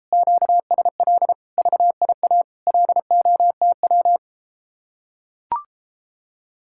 Morse code, heard as a keyed tone, about 33 wpm, 700 Hz. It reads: QSL VIA LOTW